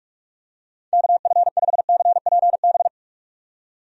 Morse code, heard as a keyed tone, about 41 wpm, 700 Hz. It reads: KF5CPB
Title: USA Call Signs - 6 Characters - 45wpm